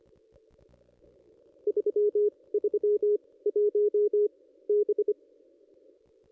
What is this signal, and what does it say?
Morse code, heard as a keyed tone, about 25 wpm, spaced as for 18 wpm, 420 Hz. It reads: 331 B
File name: contests / ss-cw-k5tr-op-n5rz -1600